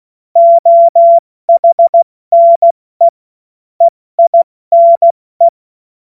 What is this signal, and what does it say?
Morse code, heard as a keyed tone, about 16 wpm, 680 Hz. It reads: OHNE EINE
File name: Morsekurs / Witze / Witz 15. November 2022 70 F80